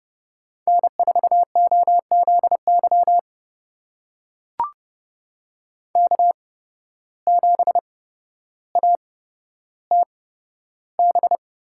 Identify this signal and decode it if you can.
Morse code, heard as a keyed tone, about 30 wpm, spaced as for 13 wpm, 700 Hz. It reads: N4OZY K7ATB
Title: USA Call Signs - 5 Characters - 8x Character Spacing - 30wpm